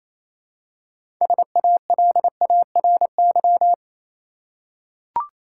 Morse code, heard as a keyed tone, about 28 wpm, 700 Hz. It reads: SALARY
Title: Six Letter Words - 28wpm